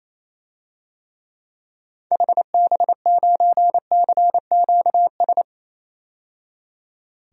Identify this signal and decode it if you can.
Morse code, heard as a keyed tone, about 28 wpm, 700 Hz. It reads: HB9CQS
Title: DXCC-100 Call Signs - 6 Characters - 28wpm